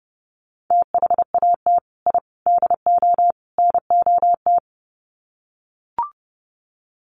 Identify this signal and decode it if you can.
Morse code, heard as a keyed tone, about 28 wpm, 700 Hz. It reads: THAT I DO NOT
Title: Sets of 4 Words 30wpm